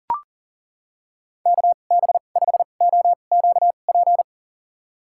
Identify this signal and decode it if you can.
Morse code, heard as a keyed tone, about 40 wpm, spaced as for 29 wpm, 700 Hz. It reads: KB5OQP
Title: USA Call Signs - 6 Characters - 2x Character Spacing - 40wpm